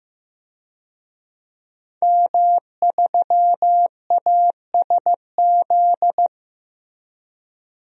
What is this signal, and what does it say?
Morse code, heard as a keyed tone, about 15 wpm, 700 Hz. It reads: M3ASZ
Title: DXCC-100 Call Signs - 5 Characters - 15wpm